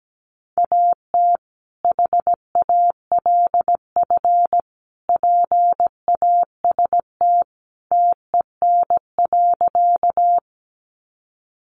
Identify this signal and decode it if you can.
Morse code, heard as a keyed tone, about 17 wpm, 700 Hz. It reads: AT HALF PAST TEN.
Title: Sentences from Top 600 Words 17wpm